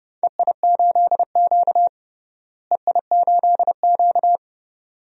Morse code, heard as a keyed tone, about 30 wpm, 700 Hz. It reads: EI8Q EI8Q
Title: DXCC-100 Call Signs - 4 Characters - Repeated 2x - 30wpm